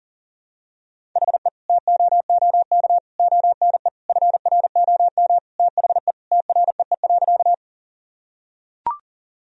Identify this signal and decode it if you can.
Morse code, heard as a keyed tone, about 40 wpm, 700 Hz. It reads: HE TOOK ONE FROM THE TREE.
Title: Sentences from Top 1000 Words - Review Part 1 - 40wpm